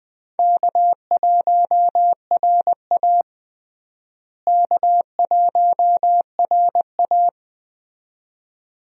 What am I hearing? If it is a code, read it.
Morse code, heard as a keyed tone, about 20 wpm, 700 Hz. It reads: K1RA K1RA
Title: USA Call Signs - 4 Characters - Repeated 2x - 20wpm